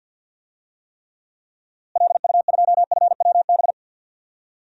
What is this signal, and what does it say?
Morse code, heard as a keyed tone, about 50 wpm, 700 Hz. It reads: LU2FWB
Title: DXCC-100 Call Signs - 6 Characters - Repeated 2x - 50wpm